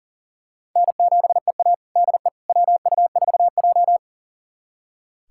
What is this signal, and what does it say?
Morse code, heard as a keyed tone, about 40 wpm, 700 Hz. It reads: N7EA DE WU4J
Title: USA Callsign DE Callsign - 4 Characters - 40wpm